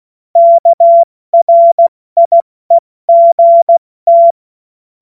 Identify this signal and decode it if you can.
Morse code, heard as a keyed tone, about 16 wpm, 680 Hz. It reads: KRIEGT
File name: Morsekurs / Witze / Witz 02 Januar 2024 70 F80